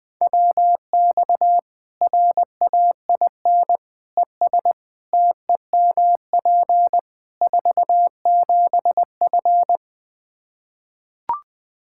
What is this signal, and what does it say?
Morse code, heard as a keyed tone, about 20 wpm, 700 Hz. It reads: WX RAIN ES TEMP 47F
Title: Practice QSO - Exchange 2 - Info - 20wpm